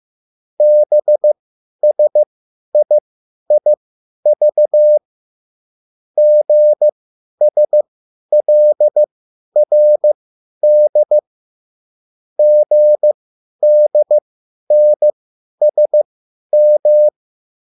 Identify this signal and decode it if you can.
Morse code, heard as a keyed tone, about 15 wpm, spaced as for 10 wpm, 600 Hz. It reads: BSIIV GSLRD GDNSM